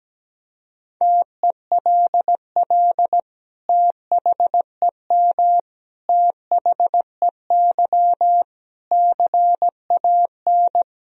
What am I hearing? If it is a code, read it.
Morse code, heard as a keyed tone, about 17 wpm, 700 Hz. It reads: TELL THEM THEY CAN
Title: Sentences from Top 1000 Words - Review Part 2 - 17wpm